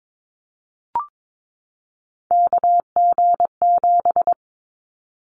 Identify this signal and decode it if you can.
Morse code, heard as a keyed tone, about 22 wpm, 700 Hz. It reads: KG7
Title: Callsign Prefixes for Top 25 DXCC - 22wpm